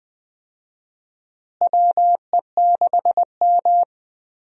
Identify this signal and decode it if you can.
Morse code, heard as a keyed tone, about 20 wpm, 700 Hz. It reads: WE6M
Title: USA Call Signs - 4 Characters - Repeated 3x - 20wpm